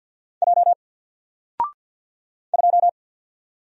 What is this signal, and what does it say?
Morse code, heard as a keyed tone, about 54 wpm, 700 Hz. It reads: J 2
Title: Single Character (Letter:Number:Punctuation:Prosign) 50wpm